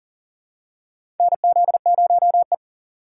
Morse code, heard as a keyed tone, about 40 wpm, 700 Hz. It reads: NZ0E
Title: USA Call Signs - 4 Characters - 3x Character Spacing - 40wpm